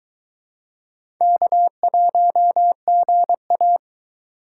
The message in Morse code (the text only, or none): K1GA